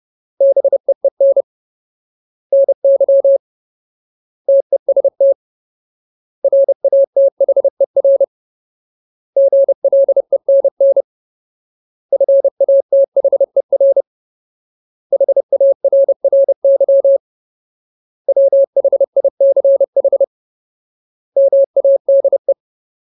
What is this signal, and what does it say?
Morse code, heard as a keyed tone, about 30 wpm, 550 Hz. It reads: BEEN NY TEST RATHER GLENN FATHER HARRY WHICH MADE
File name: Ultimate Word List / 30WPM Ultimate